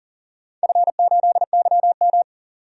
Morse code, heard as a keyed tone, about 38 wpm, 700 Hz. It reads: F8YM
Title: DXCC-100 Call Signs - 4 Characters - 40wpm